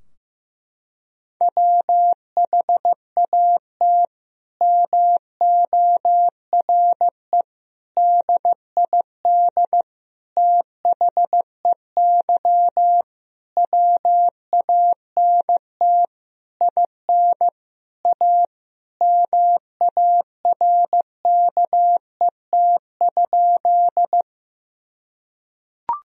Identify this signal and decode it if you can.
Morse code, heard as a keyed tone, about 15 wpm, 700 Hz. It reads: WHAT MORE DID THEY WANT IN A MARKET?
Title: Sentences from Top 1000 Words - Review Part 3 - 15wpm